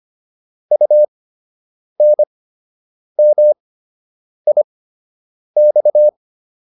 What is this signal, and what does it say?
Morse code, heard as a keyed tone, about 25 wpm, spaced as for 8 wpm, 600 Hz. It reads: UNMIX